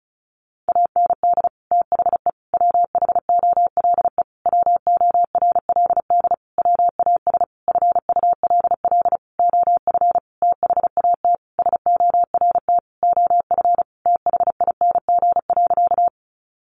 Morse code, heard as a keyed tone, about 34 wpm, 700 Hz. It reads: AND THE WHOLE WORLD WAS FULL OF THAT SORT OF THING.
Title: Sentences from Top 1300 Words 35wpm